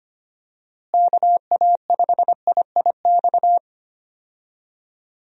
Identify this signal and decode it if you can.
Morse code, heard as a keyed tone, about 25 wpm, 700 Hz. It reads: KA5IIX
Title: USA Call Signs - 6 Characters - Repeated 3x - 25wpm